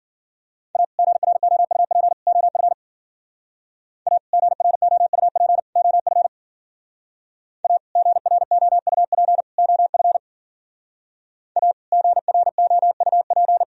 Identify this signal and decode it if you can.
Morse code, heard as a keyed tone, about 50 wpm, 700 Hz. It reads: A GROUP OF A GROUP OF A GROUP OF A GROUP